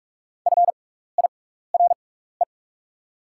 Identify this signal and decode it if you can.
Morse code, heard as a keyed tone, about 44 wpm, spaced as for 15 wpm, 700 Hz. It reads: FIRE
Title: 6x Character Spacing - Top 500 Words - 45wpm